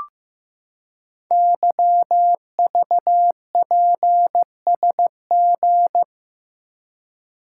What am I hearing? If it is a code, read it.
Morse code, heard as a keyed tone, about 15 wpm, 700 Hz. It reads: YVPSG
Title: Code Groups - 5 Letters - 15wpm